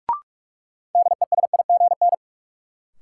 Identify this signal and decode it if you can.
Morse code, heard as a keyed tone, about 46 wpm, 700 Hz. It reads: DESIGN
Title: Sentences from Top 800 Words - New Word Drill - 45wpm